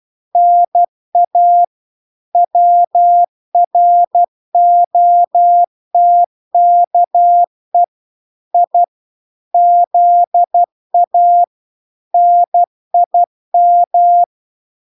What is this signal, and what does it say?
Morse code, heard as a keyed tone, about 12 wpm, 700 Hz. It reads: NA WROTKE I ZA NIM